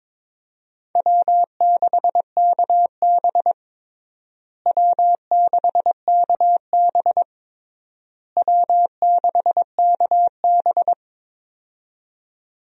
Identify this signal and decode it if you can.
Morse code, heard as a keyed tone, about 22 wpm, 700 Hz. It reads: W6KB W6KB W6KB
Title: USA Call Signs - 4 Characters - Repeated 3x - 22wpm